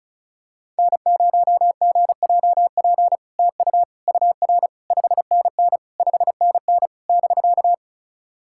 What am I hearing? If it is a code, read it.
Morse code, heard as a keyed tone, about 36 wpm, 700 Hz. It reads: N0GJP TU UR 5NN 5NN <BK>
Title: SOTA QSO Practice 35wpm